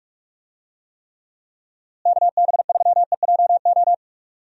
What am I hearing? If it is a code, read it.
Morse code, heard as a keyed tone, about 47 wpm, 700 Hz. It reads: KB3EJO
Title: USA Call Signs - 6 Characters - Repeated 3x - 45wpm